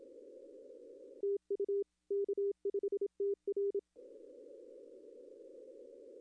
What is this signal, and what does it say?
Morse code, heard as a keyed tone, about 27 wpm, 400 Hz. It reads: TU K5TR
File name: contests / arrl-160-k5tr-op-w5tn -1300